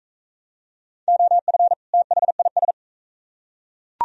Morse code, heard as a keyed tone, about 40 wpm, 700 Hz. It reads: OF THIS